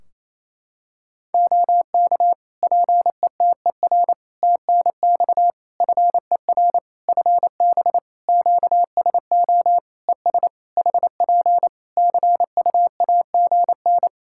Morse code, heard as a keyed tone, about 27 wpm, 700 Hz. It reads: OK PETER TNX FER FB QSO ES HP CUAGN
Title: K1USN SST QSO Practice 28wpm